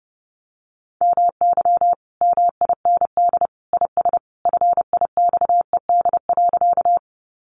Morse code, heard as a keyed tone, about 30 wpm, 700 Hz. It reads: MY MIND IS FIXED.
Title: Sentences from Top 1100 Words 30wpm